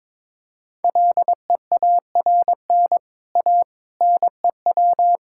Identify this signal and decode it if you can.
Morse code, heard as a keyed tone, about 22 wpm, 700 Hz. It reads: LEARN A NEW
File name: practice / 22wpm / Easy Sentences with Hints - 22wpm